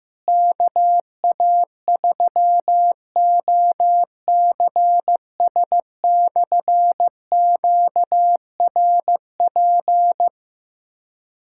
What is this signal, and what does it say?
Morse code, heard as a keyed tone, about 15 wpm, 700 Hz. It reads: KA3OCS/QRP
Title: K1USN SST QSO Practice - 15wpm